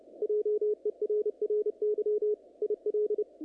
Morse code, heard as a keyed tone, about 29 wpm, 420 Hz, with radio noise. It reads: JERRY IL
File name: contests / naqp-cw-k5tr-op-k5pi- -2300